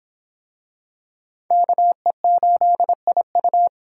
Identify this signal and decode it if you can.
Morse code, heard as a keyed tone, about 24 wpm, 700 Hz. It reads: KE8IU